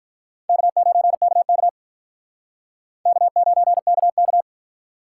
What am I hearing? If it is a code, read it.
Morse code, heard as a keyed tone, about 43 wpm, 700 Hz. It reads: K9KK K9KK
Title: Speed Racing - USA 1x2 Call Signs 35wpm